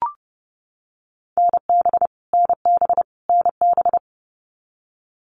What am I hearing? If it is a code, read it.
Morse code, heard as a keyed tone, about 30 wpm, 700 Hz. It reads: NB NB NB
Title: Canadian Provinces and Territories - Repeated 3x - 30wpm